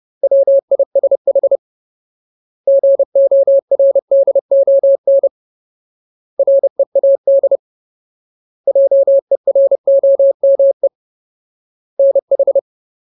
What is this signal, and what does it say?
Morse code, heard as a keyed tone, about 30 wpm, 550 Hz. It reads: WISH GORDON READ JEROME NH
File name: Ultimate Word List / 30WPM Ultimate